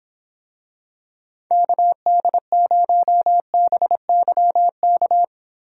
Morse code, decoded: KD0BYK

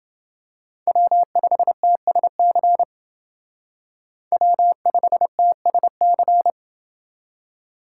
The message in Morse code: W5TSC W5TSC

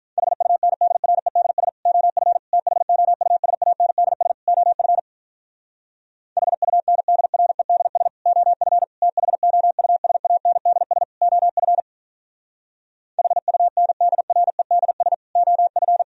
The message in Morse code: HUNDREDS OF THOUSANDS OF HUNDREDS OF THOUSANDS OF HUNDREDS OF

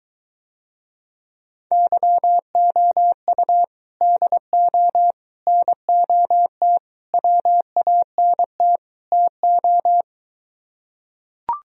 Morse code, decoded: YOU DO NOT WANT TO

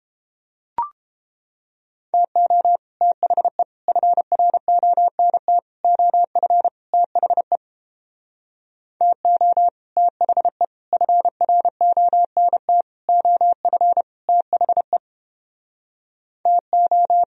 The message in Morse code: TO THE FRONT OF THE TO THE FRONT OF THE TO

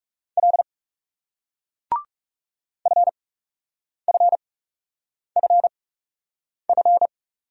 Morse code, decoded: L FFFF

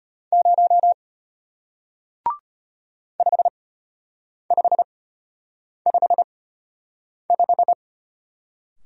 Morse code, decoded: 0 5555

